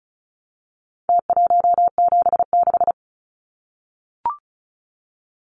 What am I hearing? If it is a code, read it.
Morse code, heard as a keyed tone, about 34 wpm, 700 Hz. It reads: T176
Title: Numbers - Contest Style - 4-Digit 35wpm